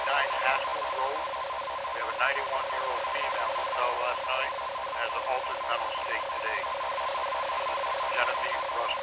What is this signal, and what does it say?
Two-tone teleprinter signal, about 450 Hz shift, with speech.